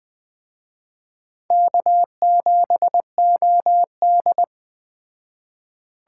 Morse code, heard as a keyed tone, about 20 wpm, 700 Hz. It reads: K7OD
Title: Speed Racing - USA 1x2 Call Signs 17wpm